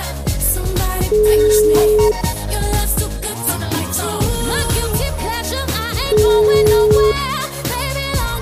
Ringback tone (Europe, 425 Hz), with noise.